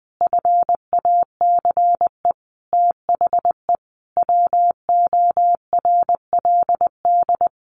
Morse code, heard as a keyed tone, about 19 wpm, 700 Hz. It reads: FACE THE WORLD
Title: Sentences from Top 1200 Words 20wpm